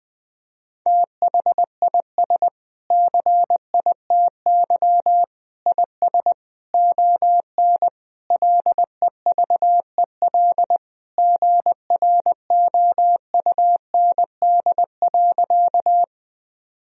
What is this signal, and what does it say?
Morse code, heard as a keyed tone, about 19 wpm, 700 Hz. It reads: THIS CITY IS ON LEVEL GROUND.
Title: Sentences from Top 1000 Words - Review Part 4 - 20wpm